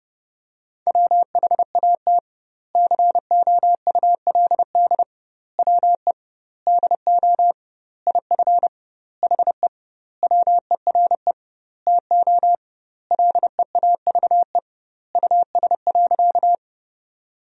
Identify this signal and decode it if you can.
Morse code, heard as a keyed tone, about 30 wpm, 700 Hz. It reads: WHAT COULD WE DO IF HE WERE TO LEAVE US.
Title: 2x Word-Spacing - Sentences from Top 400 Words - 30wpm